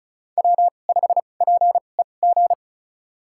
Morse code, heard as a keyed tone, about 34 wpm, spaced as for 25 wpm, 690 Hz. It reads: W5PEG
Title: USA Call Signs - 5 Characters - 2x Character Spacing - 35wpm